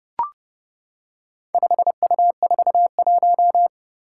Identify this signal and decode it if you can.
Morse code, heard as a keyed tone, about 32 wpm, 700 Hz. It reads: 5U41
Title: Four Challenging Letter-Number Combinations 30wpm